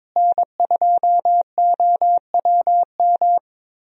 Morse code, heard as a keyed tone, about 21 wpm, 700 Hz. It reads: N2OWM
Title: USA Call Signs - 5 Characters - Repeated 3x - 22wpm